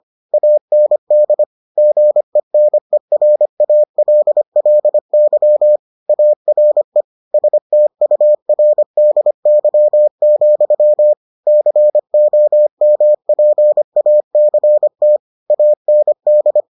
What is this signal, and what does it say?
Morse code, heard as a keyed tone, about 25 wpm, 600 Hz. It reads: AND GENERALLY ARE STURDY, COMPACT AND